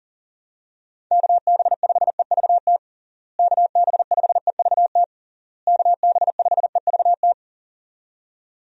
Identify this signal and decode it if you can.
Morse code, heard as a keyed tone, about 37 wpm, 700 Hz. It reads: KB5EVT KB5EVT KB5EVT
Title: USA Call Signs - 6 Characters - Repeated 3x - 40wpm